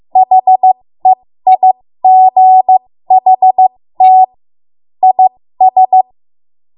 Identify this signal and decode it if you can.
Morse code, heard as a keyed tone, about 15 wpm, spaced as for 13 wpm, 750 Hz. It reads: HEIGHT IS
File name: Morse / 13WPM